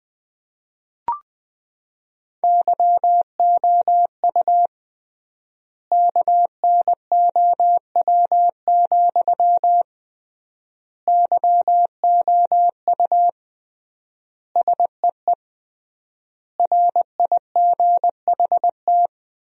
Morse code, heard as a keyed tone, about 20 wpm, 700 Hz. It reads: YOU KNOW, YOU SEE RIGHT
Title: 3x Word-Spacing - Sentences from Top 200 Words - 20wpm